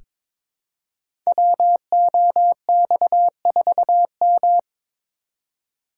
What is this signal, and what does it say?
Morse code, heard as a keyed tone, about 22 wpm, 700 Hz. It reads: WOX4M
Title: Code Groups - 5 Letter:Numbers - 22wpm